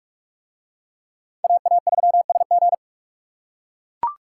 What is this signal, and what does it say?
Morse code, heard as a keyed tone, about 43 wpm, 700 Hz. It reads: AA3SG